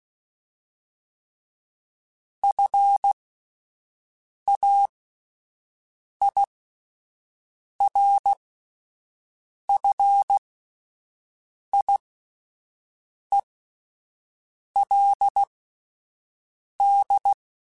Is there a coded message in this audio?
Yes: Morse code from 2.4 to 17.3 s, heard as a keyed tone, about 16 wpm, spaced as for 5 wpm, 800 Hz. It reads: FAIRFIELD